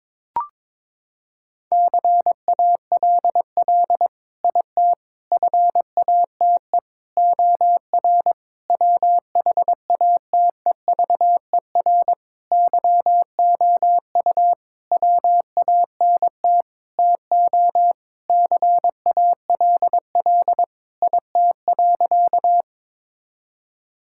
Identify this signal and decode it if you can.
Morse code, heard as a keyed tone, about 22 wpm, 700 Hz. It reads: CALL IT FATE OR WHATEVER YOU WANT TO CALL IT.
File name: Sentences with 7-Word Phrases - 22wpm